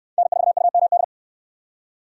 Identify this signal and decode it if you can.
Morse code, heard as a keyed tone, about 65 wpm, 700 Hz. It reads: N4UMD